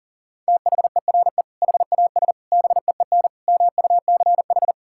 Morse code, heard as a keyed tone, about 40 wpm, 700 Hz. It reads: THERE HAS BEEN MUCH
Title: Sentences from Top 1000 Words - Review Part 5 - 40wpm